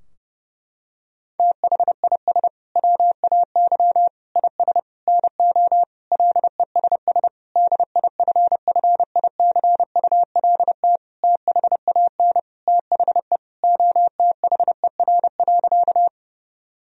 Morse code, heard as a keyed tone, about 30 wpm, 700 Hz. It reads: THIS WAY IS NO LESS DIFFICULT THAN THE OTHER.
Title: Sentences from Top 1000 Words - Review Part 2 - 30wpm